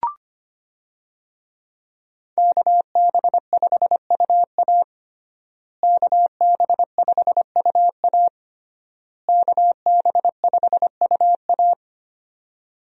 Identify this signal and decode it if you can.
Morse code, heard as a keyed tone, about 25 wpm, 700 Hz. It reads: KB5UA KB5UA KB5UA
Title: USA Call Signs - 5 Characters - Repeated 3x - 25wpm